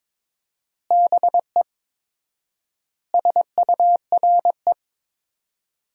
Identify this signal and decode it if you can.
Morse code, heard as a keyed tone, about 21 wpm, 700 Hz. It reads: BE SURE
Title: 4x Word-Spacing - Sentences from Top 300 Words - 22wpm